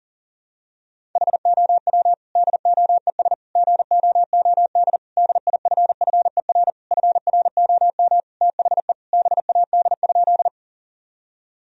Morse code, heard as a keyed tone, about 40 wpm, 700 Hz. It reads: HOW DOES GOOD DIFFER FROM THE BAD?
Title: Sentences from Top 1000 Words - Review Part 1 - 40wpm